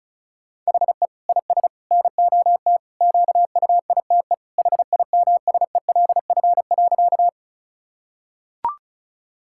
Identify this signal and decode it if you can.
Morse code, heard as a keyed tone, about 34 wpm, 700 Hz. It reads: HE IS NOT QUITE HIMSELF.
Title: Sentences from Top 1600 Words 35wpm